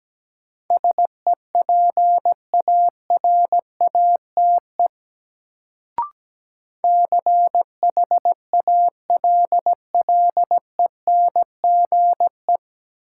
Morse code, heard as a keyed tone, about 17 wpm, 700 Hz. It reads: SEPARATE CHALLENGE
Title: Sentences from Top 1000 Words - Review Part 7 - New Word Drill - 17wpm